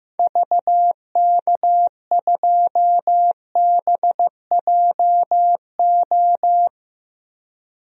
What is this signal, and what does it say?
Morse code, heard as a keyed tone, about 15 wpm, 700 Hz. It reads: VK2BJO
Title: DXCC-100 Call Signs - 6 Characters - Repeated 2x - 15wpm